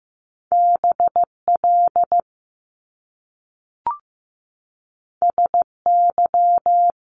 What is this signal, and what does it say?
Morse code, heard as a keyed tone, about 15 wpm, 700 Hz. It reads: BL SY
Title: Top Two Letter Combinations in Words 15wpm